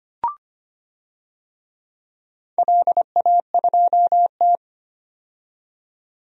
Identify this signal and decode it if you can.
Morse code, heard as a keyed tone, about 25 wpm, 700 Hz. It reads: LA2T
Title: DXCC-100 Call Signs - 4 Characters - 25wpm